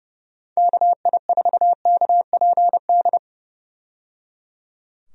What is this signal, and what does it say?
Morse code, heard as a keyed tone, about 30 wpm, 700 Hz. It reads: KI4KPD
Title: USA Call Signs - 6 Characters - 30wpm